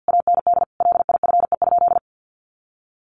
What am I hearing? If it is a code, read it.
Morse code, heard as a keyed tone, about 50 wpm, 700 Hz. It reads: AND LIFE?
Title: Sentences from Top 1300 Words 50wpm